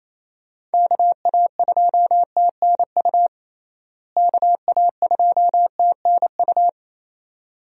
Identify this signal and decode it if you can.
Morse code, heard as a keyed tone, about 28 wpm, 700 Hz. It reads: KA2TNU KA2TNU
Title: USA Call Signs - 6 Characters - Repeated 2x - 28wpm